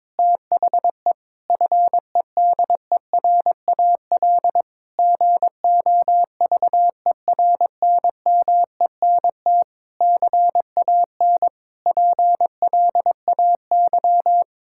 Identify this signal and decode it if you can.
Morse code, heard as a keyed tone, about 22 wpm, 700 Hz. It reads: THE FEDERAL GOVERNMENT CAN PLAY
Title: Sentences from Top 1000 Words - Review Part 5 - 22wpm